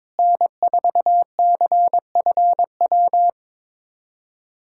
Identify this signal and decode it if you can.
Morse code, heard as a keyed tone, about 22 wpm, 700 Hz. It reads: N4CFW
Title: USA Callsign DE Callsign 22wpm